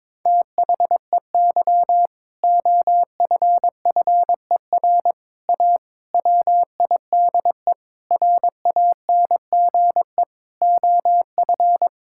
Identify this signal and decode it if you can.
Morse code, heard as a keyed tone, about 22 wpm, 700 Hz. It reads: THEY OFFER A WIDE RANGE OF
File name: practice / 22wpm / Sentences with 4-Word Phrases - 22wpm